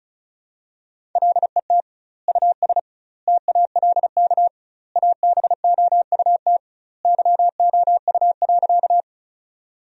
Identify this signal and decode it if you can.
Morse code, heard as a keyed tone, about 36 wpm, 700 Hz. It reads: LET US TALK ABOUT YOU.
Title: 2x Word-Spacing - Sentences from Top 400 Words - 35wpm